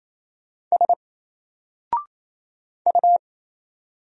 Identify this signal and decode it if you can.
Morse code, heard as a keyed tone, about 27 wpm, 700 Hz. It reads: S U
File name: Single-Letter-Number-28wpm